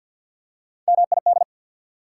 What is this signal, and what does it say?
Morse code, heard as a keyed tone, about 54 wpm, 690 Hz. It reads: MID
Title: Prefixes and Suffixes 50wpm